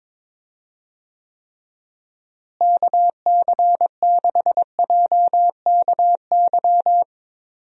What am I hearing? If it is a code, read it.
Morse code, heard as a keyed tone, about 22 wpm, 700 Hz. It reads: KC6JKY
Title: USA Call Signs - 6 Characters - Repeated 2x - 22wpm